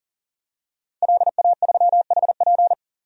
Morse code, heard as a keyed tone, about 37 wpm, 700 Hz. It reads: LA3HP
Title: DXCC-100 Call Signs - 5 Characters - 40wpm